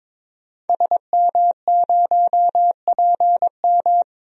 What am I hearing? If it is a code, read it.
Morse code, heard as a keyed tone, about 22 wpm, 700 Hz. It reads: SM0PM
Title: DXCC-100 Call Signs - 5 Characters - Repeated 2x - 22wpm